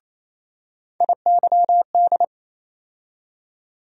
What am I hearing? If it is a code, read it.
Morse code, heard as a keyed tone, about 28 wpm, 700 Hz. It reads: IYD
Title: USA Call Sign Postfixes - 3 Letters 28wpm